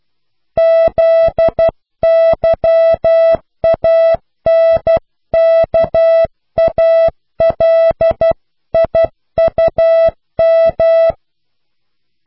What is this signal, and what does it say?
Morse code, heard as a keyed tone, about 12 wpm, 660 Hz. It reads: ZYANKALIUM
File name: cw / pseudo worden(10)